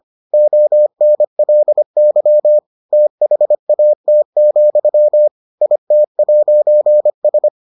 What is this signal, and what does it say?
Morse code, heard as a keyed tone, about 25 wpm, 600 Hz. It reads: ONLY THAT, IT'S